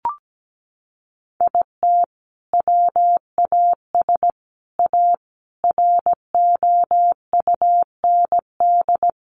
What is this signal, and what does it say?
Morse code, heard as a keyed tone, about 17 wpm, 700 Hz. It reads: IT WAS A ROUND
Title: Sentences from Top 400 Words 17wpm